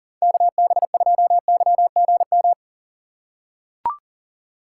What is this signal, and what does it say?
Morse code, heard as a keyed tone, about 40 wpm, 700 Hz. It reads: KB2YGM
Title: USA Call Signs - 6 Characters - 3x Character Spacing - 40wpm